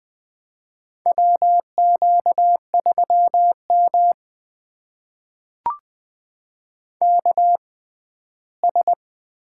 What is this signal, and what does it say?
Morse code, heard as a keyed tone, about 20 wpm, spaced as for 14 wpm, 700 Hz. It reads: WQ3M KS
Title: USA Call Signs - 4 Characters - 6x Character Spacing - 20wpm